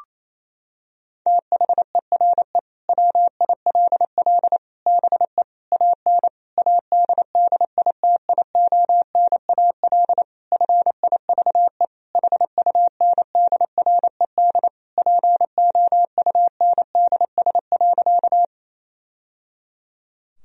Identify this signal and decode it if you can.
Morse code, heard as a keyed tone, about 28 wpm, 700 Hz. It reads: THERE WILL BE AN ADDITIONAL FIVE HUNDRED POUNDS.
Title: Sentences from Top 1000 Words - Review Part 4 - 28wpm